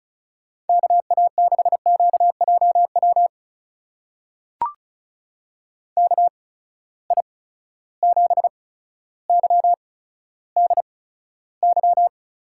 Morse code, heard as a keyed tone, about 35 wpm, spaced as for 15 wpm, 700 Hz. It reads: KA6QJW KI7YDY
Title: USA Call Signs - 6 Characters - 8x Character Spacing - 35wpm